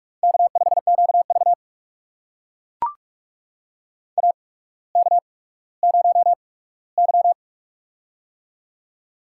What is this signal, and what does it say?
Morse code, heard as a keyed tone, about 44 wpm, spaced as for 19 wpm, 700 Hz. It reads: K5QV AK0Y
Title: USA Call Signs - 4 Characters - 8x Character Spacing - 45wpm